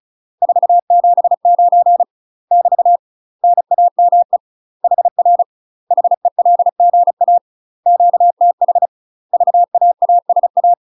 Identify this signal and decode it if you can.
Morse code, heard as a keyed tone, about 36 wpm, 700 Hz. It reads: 479 <BT> NAME HR HELGA QTH VAASA
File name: UK MASTER QSOs TRAINING / 35WPM / UK MASTER QSOs 0041